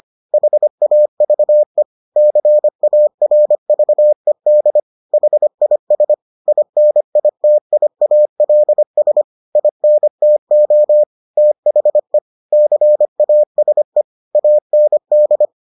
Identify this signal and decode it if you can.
Morse code, heard as a keyed tone, about 25 wpm, 600 Hz. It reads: HAVE CARVED HIS INITIALS INTO THE CASE AND